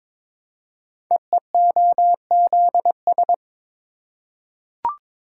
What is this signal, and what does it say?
Morse code, heard as a keyed tone, about 22 wpm, 700 Hz. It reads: EEOZS
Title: Code Groups - 5 Letters - 22wpm